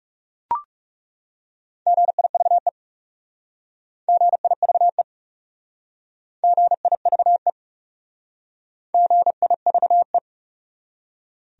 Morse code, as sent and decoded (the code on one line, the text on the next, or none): --. .. ...- . / --. .. ...- . / --. .. ...- . / --. .. ...- .
GIVE GIVE GIVE GIVE